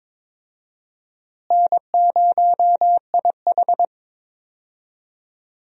Morse code, heard as a keyed tone, about 22 wpm, 700 Hz. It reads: N0IH
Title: USA Call Signs - 4 Characters - Repeated 2x - 22wpm